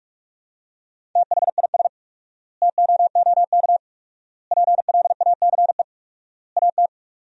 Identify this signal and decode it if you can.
Morse code, heard as a keyed tone, about 42 wpm, 700 Hz. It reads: THIS TOOK PLACE AT
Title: 4x Word-Spacing - Sentences from Top 300 Words - 45wpm